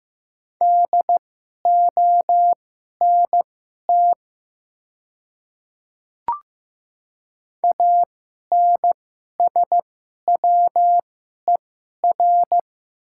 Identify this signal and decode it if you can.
Morse code, heard as a keyed tone, about 15 wpm, spaced as for 11 wpm, 700 Hz. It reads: DONT ANSWER
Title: 2x Character Spacing - Top 300 Words - 15wpm